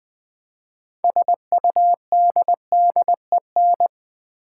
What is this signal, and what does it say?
Morse code, heard as a keyed tone, about 20 wpm, 700 Hz. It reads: SUDDEN
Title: Six Letter Words - 20wpm